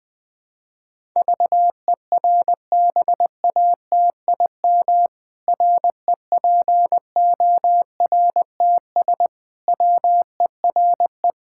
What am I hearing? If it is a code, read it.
Morse code, heard as a keyed tone, about 21 wpm, 700 Hz. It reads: VERBATIM REPORTS WERE
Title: Pangram Challenge v2 - 20wpm